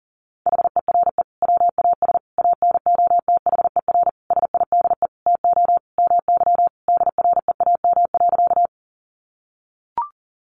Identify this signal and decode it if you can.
Morse code, heard as a keyed tone, about 40 wpm, 700 Hz. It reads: HERE WAS ANOTHER SIDE TO MY DREAM.
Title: Sentences from Top 1500 Words 40wpm